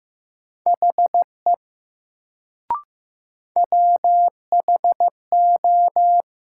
Morse code, heard as a keyed tone, about 15 wpm, 700 Hz. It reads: HE WHO
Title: Sentences from Top 100 Words - New Word Drill - 15wpm